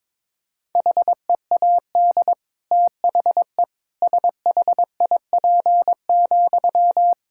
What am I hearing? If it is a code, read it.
Morse code, heard as a keyed tone, about 22 wpm, 700 Hz. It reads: HEAD THE SHIP,
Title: Sentences from Top 1000 Words - Review Part 5 - 22wpm